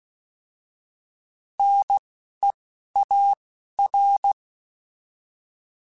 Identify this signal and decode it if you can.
Morse code, heard as a keyed tone, about 16 wpm, spaced as for 11 wpm, 800 Hz. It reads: NEAR